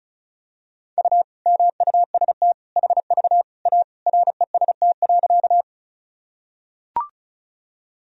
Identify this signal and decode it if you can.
Morse code, heard as a keyed tone, about 36 wpm, 700 Hz. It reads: U MUST HV A REST.
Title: Sentences from Top 500 Words and Abbreviations 35wpm